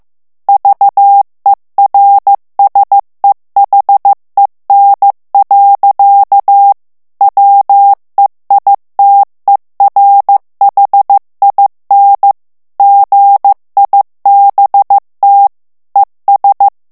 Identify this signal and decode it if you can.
Morse code, heard as a keyed tone, about 15 wpm, 800 Hz. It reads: VERSEHEN. WEITERHIN GIBT ES
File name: morse mp3 / ALLEMAND texte clair 1 vitesse 14 mots minute